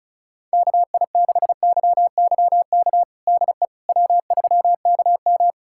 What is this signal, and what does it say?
Morse code, heard as a keyed tone, about 34 wpm, 700 Hz. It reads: KI6YYK DE W3KM
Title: USA Callsign DE Callsign 35wpm